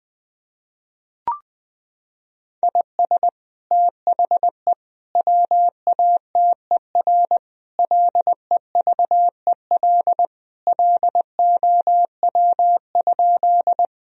Morse code, heard as a keyed tone, about 20 wpm, 700 Hz. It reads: IS THE WATER LEVEL LOW?
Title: Sentences from Top 1000 Words - Review Part 4 - 20wpm